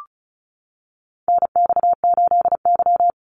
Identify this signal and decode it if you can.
Morse code, heard as a keyed tone, about 34 wpm, 700 Hz. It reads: NX8Y
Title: Call Signs - 35wpm